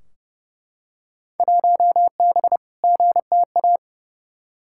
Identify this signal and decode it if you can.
Morse code, heard as a keyed tone, about 30 wpm, 700 Hz. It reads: 1B GTA
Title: Field Day Exchanges Encore - 30wpm